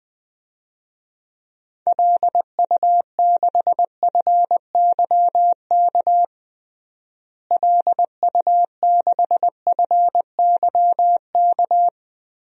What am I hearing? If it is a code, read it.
Morse code, heard as a keyed tone, about 20 wpm, 700 Hz. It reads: LU6FYK LU6FYK
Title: DXCC-100 Call Signs - 6 Characters - Repeated 3x - 20wpm